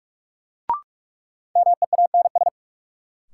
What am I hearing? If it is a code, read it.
Morse code, heard as a keyed tone, about 46 wpm, 700 Hz. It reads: MEANS
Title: Sets of 5 Words - New Word Drill - 45wpm